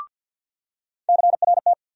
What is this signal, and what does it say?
Morse code, heard as a keyed tone, about 51 wpm, 700 Hz. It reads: CRT
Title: USA Call Sign Postfixes - 3 Letters 50wpm